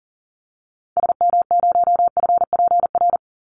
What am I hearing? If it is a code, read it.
Morse code, heard as a keyed tone, about 40 wpm, 700 Hz. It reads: SM0FPR
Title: Call Signs Repeated 3x - 40wpm